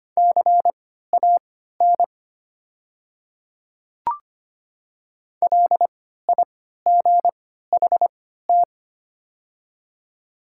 Morse code, heard as a keyed tone, about 25 wpm, spaced as for 14 wpm, 690 Hz. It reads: CAN LIGHT
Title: 3x Character Spacing - Top 200 Words - 25wpm